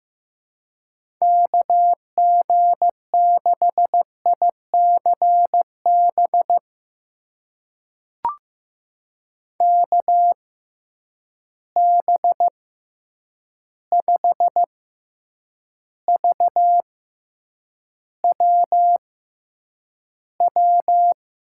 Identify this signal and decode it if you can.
Morse code, heard as a keyed tone, about 15 wpm, spaced as for 8 wpm, 700 Hz. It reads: KG6ICB KB5VWW
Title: USA Call Signs - 6x Character Spacing - 15wpm